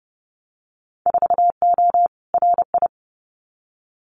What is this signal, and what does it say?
Morse code, heard as a keyed tone, about 30 wpm, 700 Hz. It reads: 4O RI
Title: Winter Field Day Exchanges Encore 30wpm